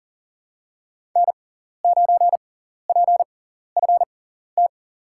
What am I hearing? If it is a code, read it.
Morse code, heard as a keyed tone, about 40 wpm, spaced as for 14 wpm, 700 Hz. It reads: N9PFT
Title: USA Call Signs - 5 Characters - 6x Character Spacing - 40wpm